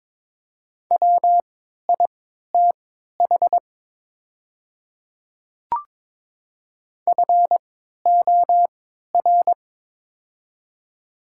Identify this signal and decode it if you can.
Morse code, heard as a keyed tone, about 22 wpm, spaced as for 12 wpm, 700 Hz. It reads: WITH FOR
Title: 3x Character Spacing - Top 100 Words - 22wpm